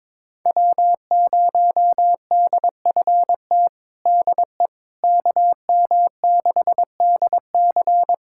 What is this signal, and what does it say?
Morse code, heard as a keyed tone, about 21 wpm, 700 Hz. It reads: W0DFT DE KM6DC
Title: USA Callsign DE Callsign - 5 Characters - 22wpm